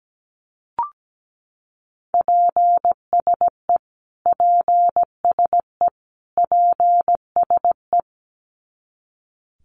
Morse code, heard as a keyed tone, about 17 wpm, 700 Hz. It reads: PSE PSE PSE
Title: Abbreviations Repeated 3x - 17wpm